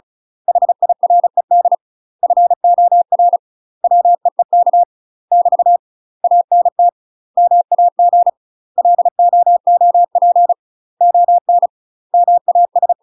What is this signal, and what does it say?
Morse code, heard as a keyed tone, about 36 wpm, 700 Hz. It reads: HIRED FOR WEEK <BT> ANT MAG LOOP ON MAS
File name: UK MASTER QSOs TRAINING / 35WPM / UK MASTER QSOs 0014